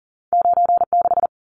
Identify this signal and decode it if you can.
Morse code, heard as a keyed tone, about 40 wpm, 700 Hz. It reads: 96
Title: Numbers - 2-Digit 40wpm